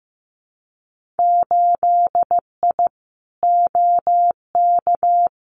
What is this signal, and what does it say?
Morse code, heard as a keyed tone, about 15 wpm, 700 Hz. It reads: 8I OK
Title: Winter Field Day Exchanges Encore 15wpm